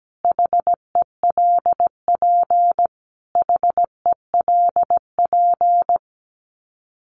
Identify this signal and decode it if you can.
Morse code, heard as a keyed tone, about 17 wpm, 700 Hz. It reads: HELP HELP
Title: Top 200 Words Repeated 17wpm